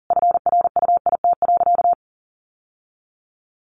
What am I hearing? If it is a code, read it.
Morse code, heard as a keyed tone, about 40 wpm, 700 Hz. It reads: FRUIT.
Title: Sentences from Top 1300 Words 40wpm